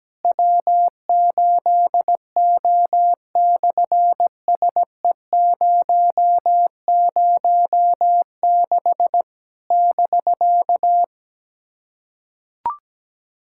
Morse code, heard as a keyed tone, about 17 wpm, 700 Hz. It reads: W8O/SE006 <BK>
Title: SOTA QSO Practice 17wpm